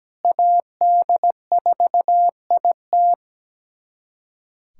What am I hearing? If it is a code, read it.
Morse code, heard as a keyed tone, about 17 wpm, 700 Hz. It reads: AD4IT
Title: USA Callsign DE Callsign - 5 Characters - 17wpm